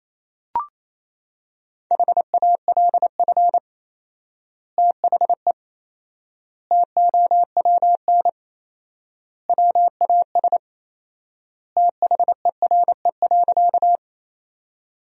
Morse code, heard as a keyed tone, about 28 wpm, 700 Hz. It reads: HALF THE TOWN WAS THERE.
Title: 4x Word-Spacing - Sentences from Top 500 Words - 28wpm